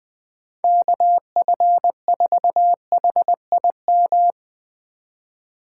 Morse code, heard as a keyed tone, about 20 wpm, 700 Hz. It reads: KF4HIM